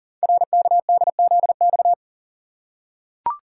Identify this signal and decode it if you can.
Morse code, heard as a keyed tone, about 40 wpm, 700 Hz. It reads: RKDZX
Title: Code Groups - 5 Letters - 40wpm